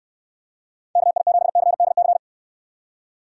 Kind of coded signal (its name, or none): Morse code